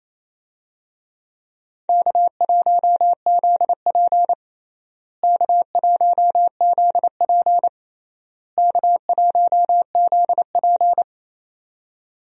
Morse code, heard as a keyed tone, about 28 wpm, 700 Hz. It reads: K1ZP K1ZP K1ZP
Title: USA Call Signs - 4 Characters - Repeated 3x - 28wpm